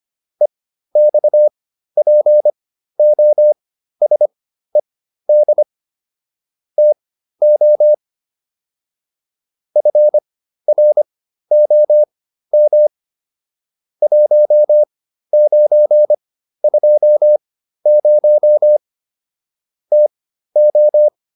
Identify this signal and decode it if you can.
Morse code, heard as a keyed tone, about 25 wpm, spaced as for 13 wpm, 600 Hz. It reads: EXPOSED TO FROM 1920 TO